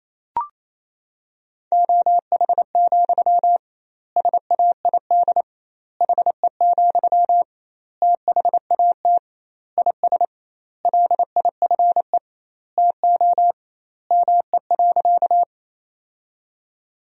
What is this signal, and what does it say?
Morse code, heard as a keyed tone, about 27 wpm, 700 Hz. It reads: OH, SAID HE, THAT IS LIFE TO ME.
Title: 2x Word-Spacing - Sentences from Top 500 Words - 28wpm